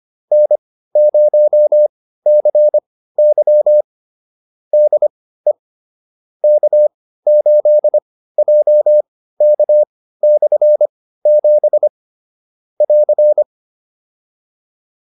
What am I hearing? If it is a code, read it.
Morse code, heard as a keyed tone, about 25 wpm, spaced as for 15 wpm, 600 Hz. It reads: N0CY DE K8JK/7 <AR>